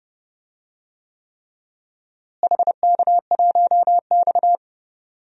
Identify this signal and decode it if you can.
Morse code, heard as a keyed tone, about 30 wpm, 700 Hz. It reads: HK1X
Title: DXCC-100 Call Signs - 4 Characters - Repeated 3x - 30wpm